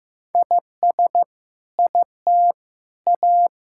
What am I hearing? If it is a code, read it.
Morse code, heard as a keyed tone, about 15 wpm, 700 Hz. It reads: IS IT A
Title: Sentences from Top 1000 Words - Review Part 6 - 15wpm